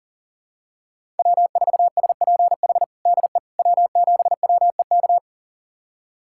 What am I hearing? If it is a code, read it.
Morse code, heard as a keyed tone, about 40 wpm, 700 Hz. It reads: W4SPH DE W7WEK
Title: USA Callsign DE Callsign - 5 Characters - 40wpm